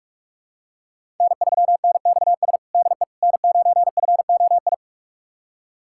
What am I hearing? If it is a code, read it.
Morse code, heard as a keyed tone, about 46 wpm, 700 Hz. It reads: N3NXS DE N9FOI
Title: USA Callsign DE Callsign - 5 Characters - 45wpm